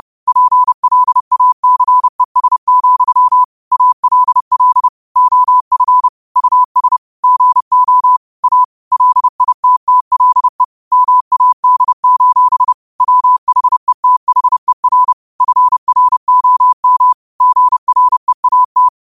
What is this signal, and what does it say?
Morse code, heard as a keyed tone, about 30 wpm, 1000 Hz. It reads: PLACES, ALL OF US GO A LITTLE MAD: WHETHER FROM GREAT